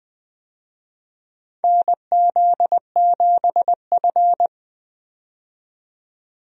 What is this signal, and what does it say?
Morse code, heard as a keyed tone, about 21 wpm, 700 Hz. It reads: NZ7F
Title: USA Call Signs - 4 Characters - 20wpm